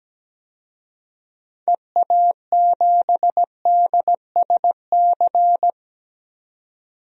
Morse code, heard as a keyed tone, about 17 wpm, 700 Hz. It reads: EA7DSC
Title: DXCC-100 Call Signs - 6 Characters - 17wpm